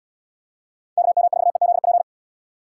Morse code, heard as a keyed tone, about 77 wpm, 700 Hz. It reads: KM4EZP